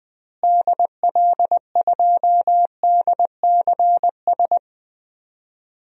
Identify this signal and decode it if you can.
Morse code, heard as a keyed tone, about 21 wpm, 700 Hz. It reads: DL2DCS
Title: DXCC-100 Call Signs - 6 Characters - Repeated 3x - 20wpm